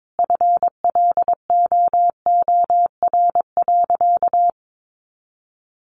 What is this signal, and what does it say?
Morse code, heard as a keyed tone, about 22 wpm, 700 Hz. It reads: FLOOR.
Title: Additional Easy Phrases 22wpm